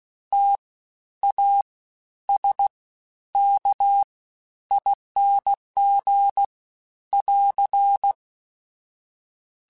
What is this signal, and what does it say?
Morse code, heard as a keyed tone, about 16 wpm, spaced as for 10 wpm, 800 Hz. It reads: TASKING <AR>
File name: Special INGs-mp3s / Special INGs-mp3s / SlowToFaster ing @15